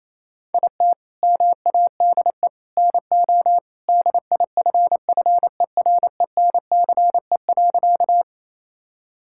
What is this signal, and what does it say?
Morse code, heard as a keyed tone, about 28 wpm, 700 Hz. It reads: IT MADE NO DIFFERENCE.
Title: Sentences from Top 1000 Words - Review Part 5 - 28wpm